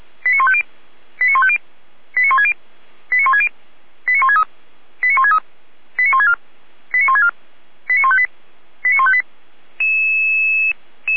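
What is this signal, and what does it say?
Five-tone selective call (ZVEI), heard as quick runs of five tones, tones of about 70 ms. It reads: 89150 89150 89160 89160 89163 89163 89164 89164 89158 89158
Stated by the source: ZVEI five-tone selective call 89121, received on SDR.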